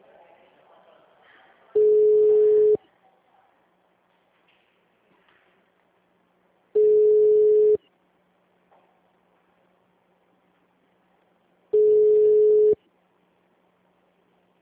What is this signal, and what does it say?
Ringback tone (Europe, 425 Hz).